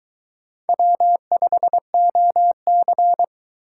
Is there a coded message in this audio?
Yes: Morse code from 0.7 to 3.2 s, heard as a keyed tone, about 23 wpm, 700 Hz. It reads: W5OC